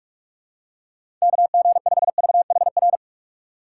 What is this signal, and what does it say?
Morse code, heard as a keyed tone, about 44 wpm, 700 Hz. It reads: KG5VHR
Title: USA Call Signs - 6 Characters - 6x Character Spacing - 45wpm